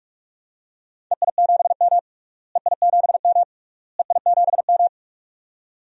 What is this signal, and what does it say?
Morse code, heard as a keyed tone, about 46 wpm, 700 Hz. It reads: EI7M EI7M EI7M
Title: DXCC-100 Call Signs - 4 Characters - Repeated 3x - 45wpm